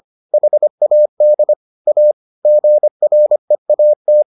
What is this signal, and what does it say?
Morse code, heard as a keyed tone, about 25 wpm, 600 Hz. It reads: HAD A GREAT